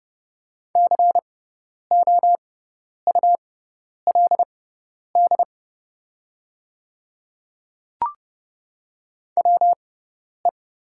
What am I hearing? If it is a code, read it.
Morse code, heard as a keyed tone, about 30 wpm, spaced as for 10 wpm, 690 Hz. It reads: COULD WE
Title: 6x Character Spacing - Top 100 Words - 30wpm